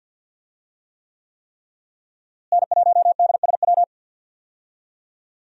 Morse code, heard as a keyed tone, about 52 wpm, 700 Hz. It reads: N1DSW
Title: USA Call Signs - 5 Characters - Repeated 3x - 50wpm